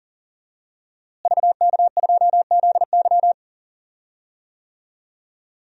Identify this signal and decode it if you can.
Morse code, heard as a keyed tone, about 37 wpm, 700 Hz. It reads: VK2ZY
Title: DXCC-100 Call Signs - 5 Characters - Repeated 2x - 40wpm